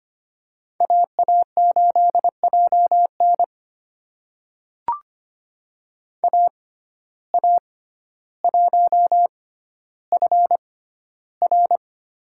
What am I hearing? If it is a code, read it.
Morse code, heard as a keyed tone, about 25 wpm, spaced as for 13 wpm, 700 Hz. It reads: AA8JN AA1FR
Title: USA Call Signs - 5 Characters - 6x Character Spacing - 25wpm